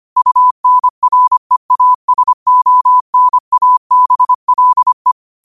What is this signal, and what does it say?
Morse code, heard as a keyed tone, about 25 wpm, 1000 Hz. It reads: UNREASONABLE
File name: qotdfiles / 25wpm